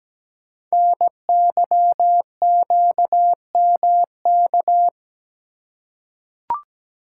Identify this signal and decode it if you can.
Morse code, heard as a keyed tone, about 17 wpm, 700 Hz. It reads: NYQMK
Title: Code Groups - 5 Letters - 17wpm